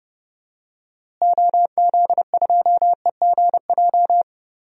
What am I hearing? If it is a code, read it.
Morse code, heard as a keyed tone, about 30 wpm, 700 Hz. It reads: OZ2EGJ
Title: DXCC-100 Call Signs - 6 Characters - Repeated 2x - 30wpm